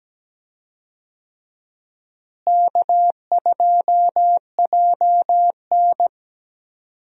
Morse code, heard as a keyed tone, about 17 wpm, 700 Hz. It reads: K2JN
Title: USA Call Signs - 4 Characters - Repeated 2x - 17wpm